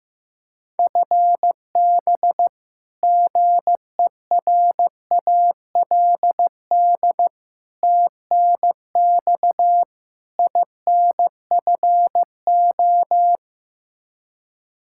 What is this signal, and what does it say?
Morse code, heard as a keyed tone, about 15 wpm, 700 Hz. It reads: FB GERALD TNX INFO
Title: Practice QSO - Exchange 3 - More Info - 15wpm